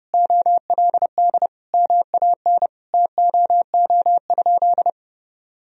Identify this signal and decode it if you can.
Morse code, heard as a keyed tone, about 32 wpm, 700 Hz. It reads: OLD MAN TOO?
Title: Sentences from Top 1000 Words - Review Part 4 - 30wpm